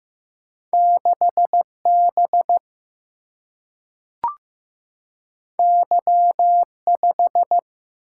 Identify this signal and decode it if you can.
Morse code, heard as a keyed tone, about 15 wpm, 700 Hz. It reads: 6 B Y 5
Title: Single-Letter Single-Number 15wpm